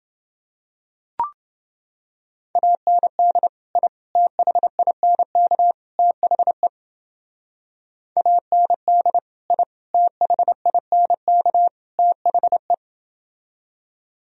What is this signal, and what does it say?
Morse code, heard as a keyed tone, about 29 wpm, 700 Hz. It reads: AND I THINK THE AND I THINK THE